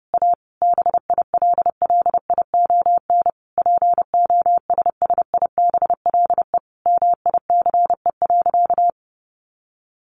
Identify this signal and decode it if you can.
Morse code, heard as a keyed tone, about 30 wpm, 700 Hz. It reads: A BILLION POSSIBLE MICE.
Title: A Sound Of Thunder 30wpm